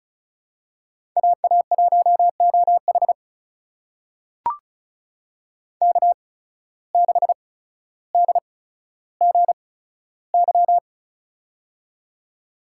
Morse code, heard as a keyed tone, about 36 wpm, spaced as for 15 wpm, 700 Hz. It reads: AA1OH K6DGY
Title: USA Call Signs - 8x Character Spacing - 35wpm